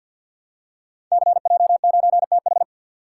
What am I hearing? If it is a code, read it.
Morse code, heard as a keyed tone, about 48 wpm, 700 Hz. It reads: CJ9TH
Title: Code Groups - 5 Letter:Numbers - 50wpm